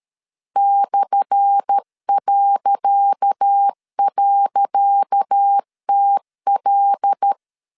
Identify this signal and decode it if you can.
Morse code, heard as a keyed tone, about 13 wpm, 800 Hz. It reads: /..TL
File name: LEZIONI / 10 SIGNS / 10C DOT 12wpm 2451